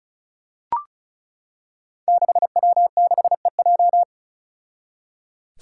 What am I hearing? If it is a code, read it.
Morse code, heard as a keyed tone, about 36 wpm, 700 Hz. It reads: 6W6EJ
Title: Five Challenging Letter-Number Combinations 35wpm